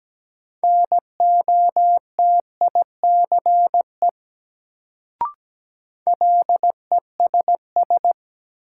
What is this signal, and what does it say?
Morse code, heard as a keyed tone, about 17 wpm, 700 Hz. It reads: NOTICE LESS
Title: Sentences from Top 400 Words - New Word Drill - 17wpm